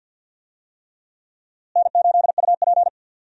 Morse code, heard as a keyed tone, about 48 wpm, 700 Hz. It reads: N8UP
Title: USA Call Signs - 4 Characters - Repeated 3x - 50wpm